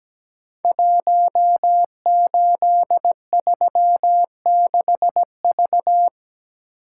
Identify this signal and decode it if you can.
Morse code, heard as a keyed tone, about 17 wpm, 700 Hz. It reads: 1836V
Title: Code Groups - 5 Letter:Numbers - 17wpm